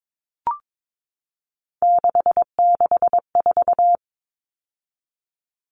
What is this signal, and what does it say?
Morse code, heard as a keyed tone, about 22 wpm, 700 Hz. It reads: 664
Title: Common Cut Numbers - 3-Digit 22wpm